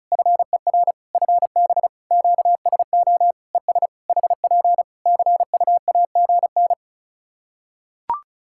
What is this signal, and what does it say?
Morse code, heard as a keyed tone, about 34 wpm, 700 Hz. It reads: FER FB QSO ES HP CUAGN
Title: Practice QSO - Exchange 4 - Ending - 35wpm